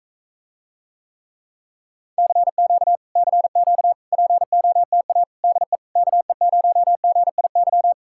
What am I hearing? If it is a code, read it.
Morse code, heard as a keyed tone, about 46 wpm, 700 Hz. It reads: CQ CQ POTA DE KE0GIY